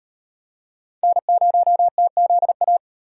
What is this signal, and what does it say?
Morse code, heard as a keyed tone, about 34 wpm, 700 Hz. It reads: N0TZA